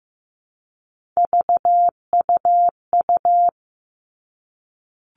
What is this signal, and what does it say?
Morse code, heard as a keyed tone, about 15 wpm, 700 Hz. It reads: VUU
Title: All Cut Numbers - 3-Digit 15wpm